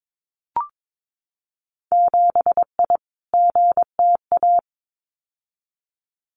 Morse code, heard as a keyed tone, about 22 wpm, 700 Hz. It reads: 7I GTA
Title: Winter Field Day Exchanges Encore 22wpm